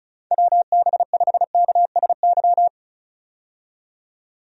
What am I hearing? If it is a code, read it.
Morse code, heard as a keyed tone, about 36 wpm, 700 Hz. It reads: WB5KSY
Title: USA Call Signs - 6 Characters - 35wpm